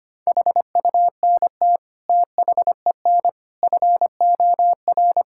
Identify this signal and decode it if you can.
Morse code, heard as a keyed tone, about 25 wpm, 700 Hz. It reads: HUNT THEN FOR
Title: Sentences from Top 1600 Words 25wpm